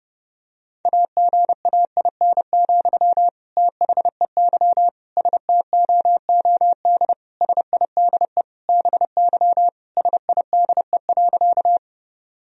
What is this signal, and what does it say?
Morse code, heard as a keyed tone, about 30 wpm, 700 Hz. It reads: AGAIN, THEY STOOD SIDE BY SIDE.
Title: Sentences from Top 1000 Words - Review Part 2 - 30wpm